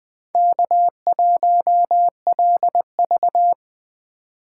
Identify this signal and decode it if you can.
Morse code, heard as a keyed tone, about 20 wpm, 700 Hz. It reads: K1LV